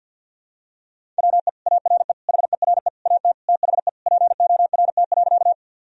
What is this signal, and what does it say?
Morse code, heard as a keyed tone, about 51 wpm, 700 Hz. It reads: WE ARE HERE AT THE PORT.
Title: Sentences from Top 1000 Words - Review Part 1 - 50wpm